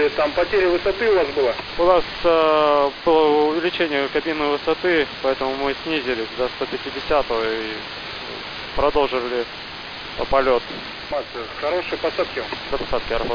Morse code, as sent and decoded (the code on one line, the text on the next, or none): none